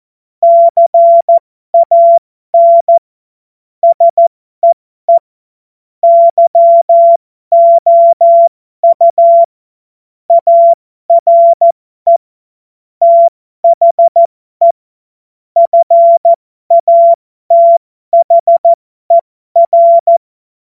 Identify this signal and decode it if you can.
Morse code, heard as a keyed tone, about 14 wpm, spaced as for 12 wpm, 680 Hz. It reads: CAN SEE YOU ARE THE FATHER
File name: Morsekurs / Witze